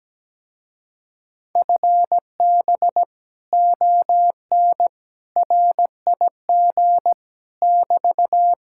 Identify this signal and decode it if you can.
Morse code, heard as a keyed tone, about 17 wpm, 700 Hz. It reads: FB ON RIG <BT>
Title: Speed-Racing Common QSO Phrases Encore - 17wpm